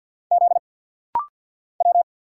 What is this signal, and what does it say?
Morse code, heard as a keyed tone, about 49 wpm, 700 Hz. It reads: Z W
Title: Single Letter-Number Alt - Rapid-Fire 50wpm